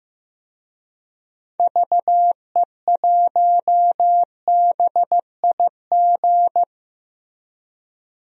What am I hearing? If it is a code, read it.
Morse code, heard as a keyed tone, about 15 wpm, 700 Hz. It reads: VE1BIG
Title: DXCC-100 Call Signs - 6 Characters - Repeated 3x - 15wpm